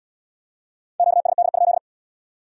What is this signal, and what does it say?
Morse code, heard as a keyed tone, about 75 wpm, 700 Hz. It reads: QSD?